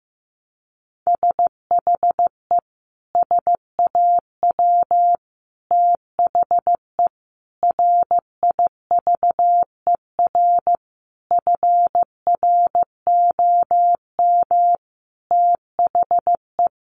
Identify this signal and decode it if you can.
Morse code, heard as a keyed tone, about 15 wpm, 700 Hz. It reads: SHE SAW THE RIVER FROM THE
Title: Sentences from Top 1200 Words 15wpm